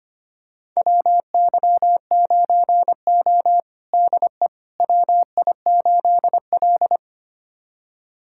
Morse code, heard as a keyed tone, about 25 wpm, 700 Hz. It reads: WY9O DE WI8L
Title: USA Callsign DE Callsign - 4 Characters - 25wpm